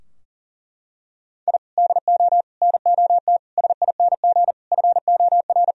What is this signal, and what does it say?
Morse code, heard as a keyed tone, about 37 wpm, 700 Hz. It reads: I DO NOT SING FOR
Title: Sentences from Top 1000 Words - Review Part 2 - 40wpm